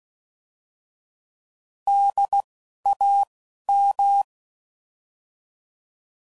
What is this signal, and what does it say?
Morse code, heard as a keyed tone, about 16 wpm, 800 Hz. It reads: D A M
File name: Prefixes / By First Letter / Ds